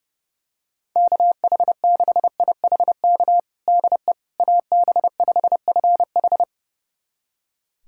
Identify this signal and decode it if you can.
Morse code, heard as a keyed tone, about 30 wpm, 700 Hz. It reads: KH6IHK DE AB5FH
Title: USA Callsign DE Callsign 30wpm